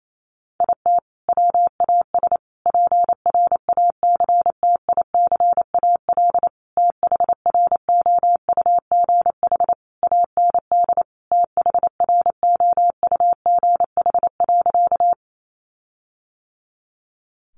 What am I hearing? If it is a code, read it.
Morse code, heard as a keyed tone, about 28 wpm, 700 Hz. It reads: IT WAS PRACTICAL THROUGH AND THROUGH.
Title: Sentences from Top 1400 Words 28wpm